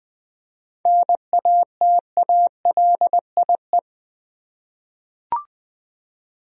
Morse code, heard as a keyed tone, about 20 wpm, 700 Hz. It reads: NATALIE
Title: USA Common Names - 20wpm